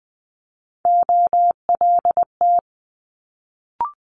Morse code, heard as a keyed tone, about 20 wpm, 700 Hz. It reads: OLT
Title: Multi-letter Phonemes Quick Fire 20wpm